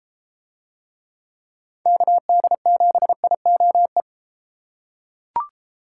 Morse code, heard as a keyed tone, about 31 wpm, 700 Hz. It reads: KD7IOE